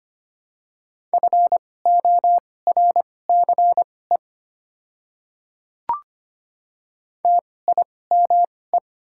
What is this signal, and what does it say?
Morse code, heard as a keyed tone, about 25 wpm, spaced as for 18 wpm, 700 Hz. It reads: FORCE TIME
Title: 2x Character Spacing - Top 500 Words - 25wpm